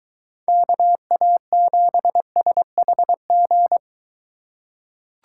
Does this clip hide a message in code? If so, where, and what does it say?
Yes: Morse code from 0.5 to 3.8 s, heard as a keyed tone, about 23 wpm, 700 Hz. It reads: KA7SHG